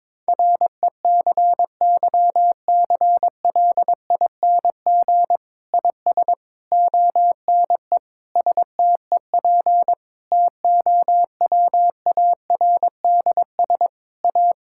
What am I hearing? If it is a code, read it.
Morse code, heard as a keyed tone, about 21 wpm, 700 Hz. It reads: RECYCLING IS ONE STEP TOWARDS A
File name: Difficult Sentences with Hints - 22wpm